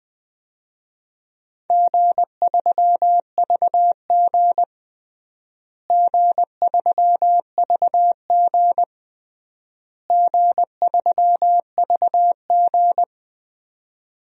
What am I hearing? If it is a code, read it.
Morse code, heard as a keyed tone, about 20 wpm, 700 Hz. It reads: G3VG G3VG G3VG
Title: DXCC-100 Call Signs - 4 Characters - Repeated 3x - 20wpm